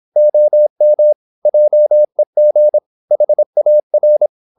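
Morse code, heard as a keyed tone, about 26 wpm, 600 Hz. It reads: OM JEG HAR
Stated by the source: High-speed (QRQ) Morse at 26 wpm.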